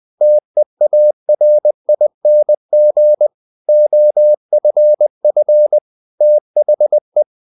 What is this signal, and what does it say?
Morse code, heard as a keyed tone, about 20 wpm, 600 Hz. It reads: TEARING OFF THE
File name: the old man and the sea part 2 20wpm